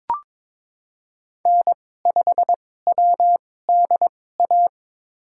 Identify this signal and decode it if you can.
Morse code, heard as a keyed tone, about 22 wpm, spaced as for 16 wpm, 700 Hz. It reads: N5WDA
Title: USA Call Signs - 5 Characters - 2x Character Spacing - 22wpm